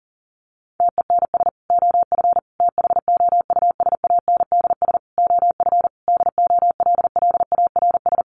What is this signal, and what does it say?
Morse code, heard as a keyed tone, about 40 wpm, 700 Hz. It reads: TENS OF THOUSANDS OF DOLLARS
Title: Sets of 5 Words Encore2 40wpm